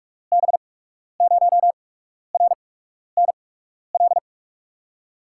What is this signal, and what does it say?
Morse code, heard as a keyed tone, about 44 wpm, spaced as for 12 wpm, 700 Hz. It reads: B0RNL
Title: USA Call Signs - 6 Characters - 8x Character Spacing - 45wpm